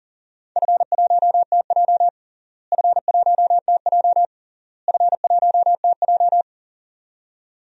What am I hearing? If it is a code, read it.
Morse code, heard as a keyed tone, about 40 wpm, 700 Hz. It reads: F1TJ F1TJ F1TJ
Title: DXCC-100 Call Signs - 4 Characters - Repeated 3x - 40wpm